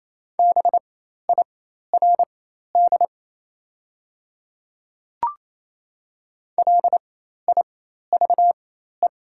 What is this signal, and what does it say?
Morse code, heard as a keyed tone, about 28 wpm, spaced as for 13 wpm, 690 Hz. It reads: BIRD LIVE
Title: 4x Character Spacing - Top 500 Words - 28wpm